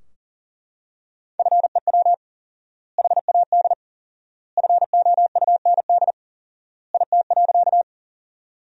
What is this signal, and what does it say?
Morse code, heard as a keyed tone, about 40 wpm, 700 Hz. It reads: FEW HAD FOUND IT.
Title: 4x Word-Spacing - Sentences from Top 300 Words - 40wpm